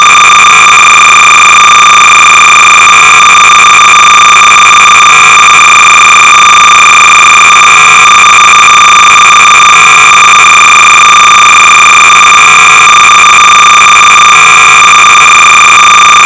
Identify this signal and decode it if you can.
Cassette data (Kansas City type).
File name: sorcerer / 300baud / Head-on-Collision-(1980)(L. Andres)(Quality Software)-Sorcerer-300baud